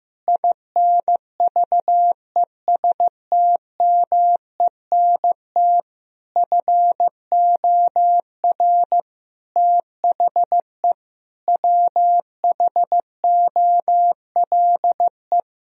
Morse code, heard as a keyed tone, about 15 wpm, 700 Hz. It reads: INVESTMENT FOR THE WHOLE
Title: Sentences from Top 1000 Words - Review Part 4 - 15wpm